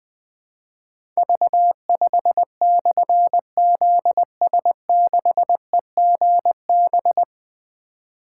Morse code, heard as a keyed tone, about 20 wpm, 700 Hz. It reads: V5/ZS6EGB
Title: Difficult Call Signs 20wpm